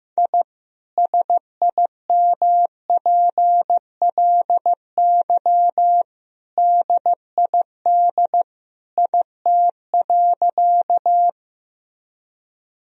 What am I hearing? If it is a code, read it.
Morse code, heard as a keyed tone, about 15 wpm, 700 Hz. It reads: I SIMPLY DID IT.
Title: Sentences from Top 1000 Words - Review Part 4 - 15wpm